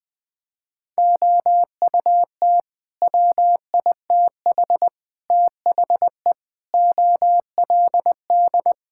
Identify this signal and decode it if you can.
Morse code, heard as a keyed tone, about 21 wpm, 700 Hz. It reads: OUT WITH THE OLD
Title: Sets of 9 Words - 20wpm